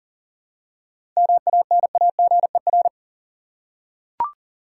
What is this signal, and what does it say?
Morse code, heard as a keyed tone, about 40 wpm, 700 Hz. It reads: MANAGER
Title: Seven Letter Words - 40wpm